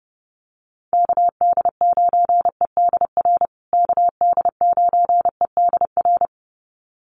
Morse code, heard as a keyed tone, about 30 wpm, 700 Hz. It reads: KD9EDR KD9EDR
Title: USA Call Signs Repeated 2x 30wpm